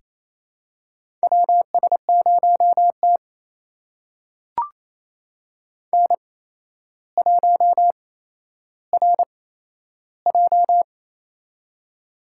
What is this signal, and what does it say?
Morse code, heard as a keyed tone, about 28 wpm, spaced as for 12 wpm, 700 Hz. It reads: WS0T N1RJ
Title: USA Call Signs - 4 Characters - 8x Character Spacing - 28wpm